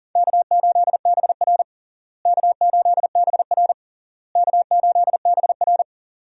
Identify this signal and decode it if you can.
Morse code, heard as a keyed tone, about 40 wpm, 700 Hz. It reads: K8BR K8BR K8BR
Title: USA Call Signs - 4 Characters - Repeated 3x - 40wpm